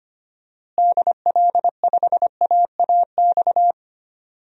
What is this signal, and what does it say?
Morse code, heard as a keyed tone, about 25 wpm, 700 Hz. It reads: DL5AAX
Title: DXCC-100 Call Signs - 6 Characters - Repeated 2x - 25wpm